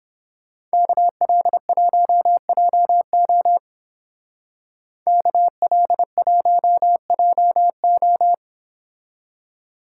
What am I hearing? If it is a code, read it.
Morse code, heard as a keyed tone, about 31 wpm, 700 Hz. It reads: KL1JO KL1JO